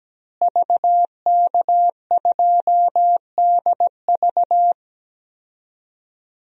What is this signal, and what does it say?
Morse code, heard as a keyed tone, about 17 wpm, 700 Hz. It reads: VK2DV
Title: DXCC-100 Call Signs - 5 Characters - Repeated 2x - 17wpm